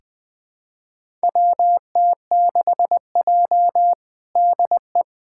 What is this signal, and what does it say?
Morse code, heard as a keyed tone, about 20 wpm, 700 Hz. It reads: WT6J DE
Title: USA Callsign DE Callsign - 4 Characters - 20wpm